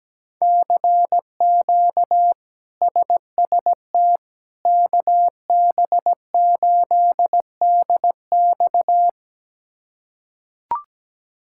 Morse code, heard as a keyed tone, about 17 wpm, 700 Hz. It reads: CQ SST KB8DX
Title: K1USN SST QSO Practice - 17wpm